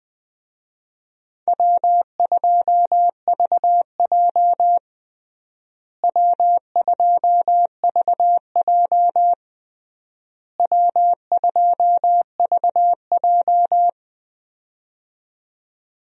Morse code, heard as a keyed tone, about 20 wpm, 700 Hz. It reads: W2VJ W2VJ W2VJ
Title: USA Call Signs - 4 Characters - Repeated 3x - 20wpm